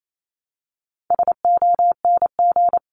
Morse code, heard as a keyed tone, about 28 wpm, 700 Hz. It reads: SONG
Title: Top 500 Words 28wpm